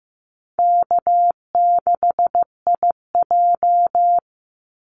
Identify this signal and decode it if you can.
Morse code, heard as a keyed tone, about 15 wpm, 700 Hz. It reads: K6IJ
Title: Easy Call Signs - 15wpm